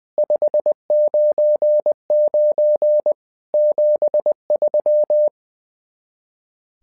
Morse code, heard as a keyed tone, about 20 wpm, 600 Hz. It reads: 599 73
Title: SOTA QSO Practice 20wpm